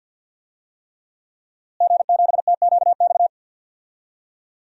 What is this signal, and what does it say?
Morse code, heard as a keyed tone, about 51 wpm, 700 Hz. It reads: G7TQX
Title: DXCC-100 Call Signs - 5 Characters - Repeated 3x - 50wpm